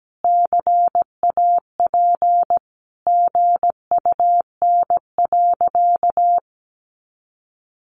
Morse code, heard as a keyed tone, about 17 wpm, 700 Hz. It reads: CAP GUN.
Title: A Sound Of Thunder 17wpm